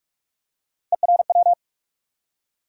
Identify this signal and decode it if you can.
Morse code, heard as a keyed tone, about 42 wpm, 700 Hz. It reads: ERW
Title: USA Call Sign Postfixes - 3 Letters 45wpm